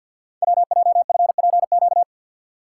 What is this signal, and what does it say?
Morse code, heard as a keyed tone, about 50 wpm, 700 Hz. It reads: WJFPQ
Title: Code Groups - 5 Letters - 50wpm